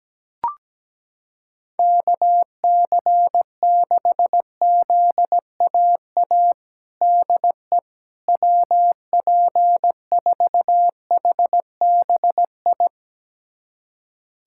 Morse code, heard as a keyed tone, about 17 wpm, 700 Hz. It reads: KC6ZAA DE WP4HBI
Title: USA Callsign DE Callsign - 6 Characters - 17wpm